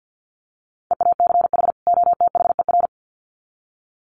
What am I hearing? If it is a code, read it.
Morse code, heard as a keyed tone, about 50 wpm, 700 Hz. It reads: EACH OTHER
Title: Sets of 2 Words 50wpm